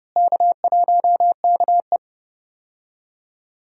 Morse code, heard as a keyed tone, about 30 wpm, 700 Hz. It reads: K1KE
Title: USA Call Signs - 4 Characters - Repeated 3x - 30wpm